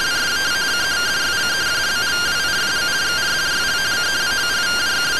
Two-tone teleprinter signal, about 170 Hz shift.